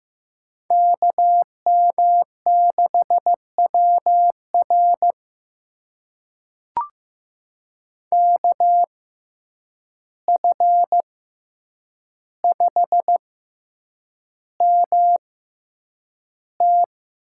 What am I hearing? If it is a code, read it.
Morse code, heard as a keyed tone, about 15 wpm, spaced as for 8 wpm, 700 Hz. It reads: KM6WR KF5MT
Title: USA Call Signs - 5 Characters - 6x Character Spacing - 15wpm